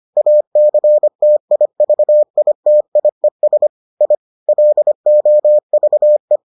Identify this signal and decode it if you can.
Morse code, heard as a keyed tone, about 25 wpm, 600 Hz. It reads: ACTIVITIES I LOVE